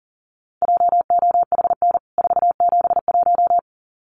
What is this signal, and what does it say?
Morse code, heard as a keyed tone, about 40 wpm, 700 Hz. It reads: JOHN 471
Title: CWT Exchange 40wpm